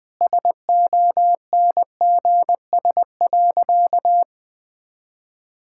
Morse code, heard as a keyed tone, about 21 wpm, 700 Hz. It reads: SONGS.
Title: Sentences with 4-Word Phrases - 20wpm